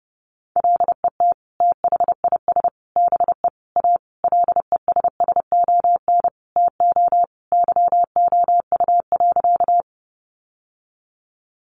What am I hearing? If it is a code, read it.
Morse code, heard as a keyed tone, about 30 wpm, 700 Hz. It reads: LET THIS BE A LESSON TO YOU.
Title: Sentences from Top 1200 Words 30wpm